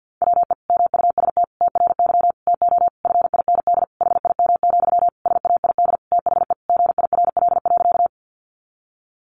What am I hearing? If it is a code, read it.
Morse code, heard as a keyed tone, about 50 wpm, 700 Hz. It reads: WE MUST TRY TO FIND HIM, SAID THE GIRL.
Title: Sentences from Top 400 Words 50wpm